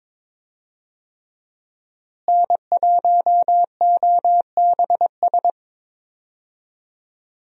Morse code, heard as a keyed tone, about 22 wpm, 700 Hz. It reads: N1OBS
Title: USA Call Signs - 5 Characters - Repeated 3x - 22wpm